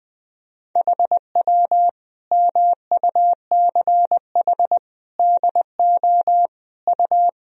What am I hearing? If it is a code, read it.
Morse code, heard as a keyed tone, about 20 wpm, 700 Hz. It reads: HW MUCH DO U
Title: Sentences from Top 200 Words and Abbreviations 20wpm